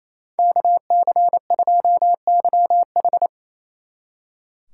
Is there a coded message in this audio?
Yes: Morse code from 0.4 to 3.3 s, heard as a keyed tone, about 37 wpm, 700 Hz. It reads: KC2YH